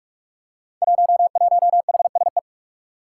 Morse code, heard as a keyed tone, about 44 wpm, 700 Hz. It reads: 11HSE
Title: Five Challenging Letter-Number Combinations 45wpm